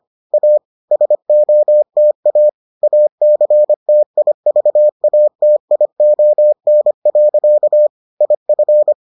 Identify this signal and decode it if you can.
Morse code, heard as a keyed tone, about 25 wpm, 600 Hz. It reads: A SOTA ACTIVATION. IF